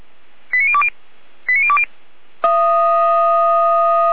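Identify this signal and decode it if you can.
Five-tone selective call (ZVEI), heard as quick runs of five tones, tones of about 80 ms. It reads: 89020 89020
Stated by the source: ZVEI five-tone selective call received on an SDR.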